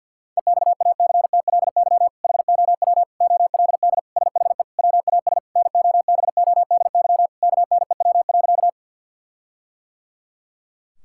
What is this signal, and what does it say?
Morse code, heard as a keyed tone, about 50 wpm, 700 Hz. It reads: EXACTLY HOW OLD SHE WAS NOBODY KNEW.
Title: Sentences from Top 1600 Words 50wpm